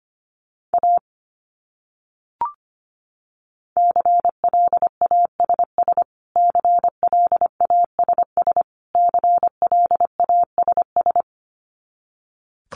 Morse code, heard as a keyed tone, about 25 wpm, 700 Hz. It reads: A CLASS CLASS CLASS
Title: Top 500 Words Repeated 25wpm